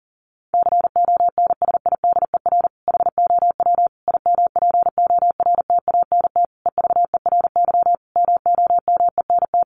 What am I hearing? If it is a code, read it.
Morse code, heard as a keyed tone, about 40 wpm, 700 Hz. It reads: CONSIDER HOW IMPORTANT EVERY MOMENT
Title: Sentences from Top 1000 Words 40wpm